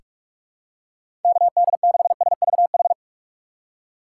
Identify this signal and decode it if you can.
Morse code, heard as a keyed tone, about 45 wpm, 700 Hz. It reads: KD6SVH
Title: USA Call Signs - 6 Characters - 6x Character Spacing - 45wpm